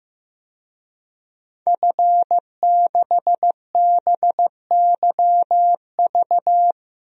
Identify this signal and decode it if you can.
Morse code, heard as a keyed tone, about 15 wpm, 700 Hz. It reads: F6BYV
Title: DXCC-100 Call Signs - 5 Characters - Repeated 3x - 15wpm